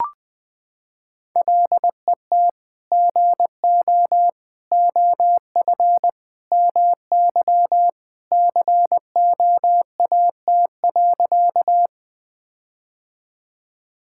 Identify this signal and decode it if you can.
Morse code, heard as a keyed tone, about 20 wpm, 700 Hz. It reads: LET GO OF MY COAT.
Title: Sentences from Top 1600 Words 20wpm